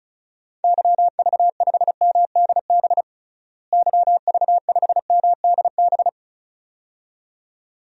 Morse code, heard as a keyed tone, about 36 wpm, 700 Hz. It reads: YV5MDB YV5MDB
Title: DXCC-100 Call Signs - 6 Characters - Repeated 2x - 35wpm